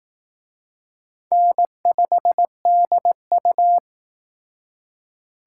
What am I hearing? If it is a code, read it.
Morse code, heard as a keyed tone, about 19 wpm, 700 Hz. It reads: N5DU
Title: Speed Racing - USA 1x2 Call Signs 15wpm